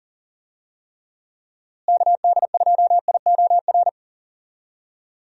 Morse code, heard as a keyed tone, about 39 wpm, 700 Hz. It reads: KD2IOR
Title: USA Call Signs - 6 Characters - Repeated 2x - 40wpm